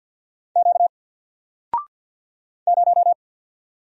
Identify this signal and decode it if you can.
Morse code, heard as a keyed tone, about 50 wpm, 700 Hz. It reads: Q 0
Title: Single Letter-Number Alt 50wpm